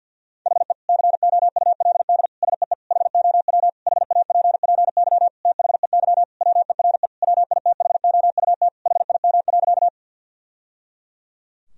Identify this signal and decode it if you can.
Morse code, heard as a keyed tone, about 52 wpm, 700 Hz. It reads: HE COULD SEE HOW HAPPY THEY WERE WITHOUT HIM.
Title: Sentences from Top 1000 Words - Review Part 3 - 50wpm